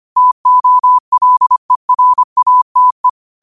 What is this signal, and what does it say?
Morse code, heard as a keyed tone, about 25 wpm, 1000 Hz. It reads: TOLERATE